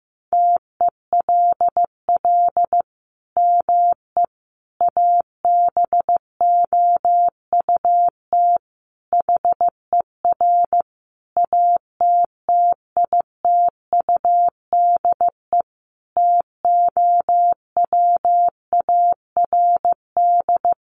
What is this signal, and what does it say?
Morse code, heard as a keyed tone, about 15 wpm, 700 Hz. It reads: TELL ME ABOUT HER ATTITUDE TOWARD
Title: Sentences from Top 1300 Words 15wpm